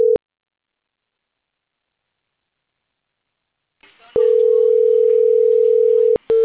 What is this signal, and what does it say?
Ringback tone (North America), 440 + 480 Hz.
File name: CENTRAL CAEA / out agcaea02 - .41064